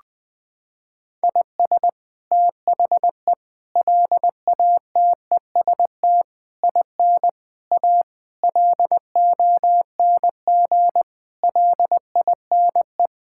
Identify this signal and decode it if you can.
Morse code, heard as a keyed tone, about 20 wpm, 700 Hz. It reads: IS THE LATEST IN A LONG LINE
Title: Sets of 8 Words - 20wpm